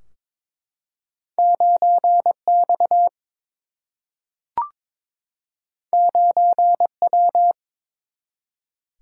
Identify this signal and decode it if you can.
Morse code, heard as a keyed tone, about 23 wpm, 700 Hz. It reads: 9 X 9 W
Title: Single-Letter Single-Number 22wpm